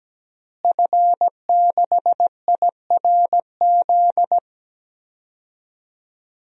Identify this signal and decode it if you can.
Morse code, heard as a keyed tone, about 17 wpm, 700 Hz. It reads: F6IRZ
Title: DXCC-100 Call Signs - 5 Characters - Repeated 2x - 17wpm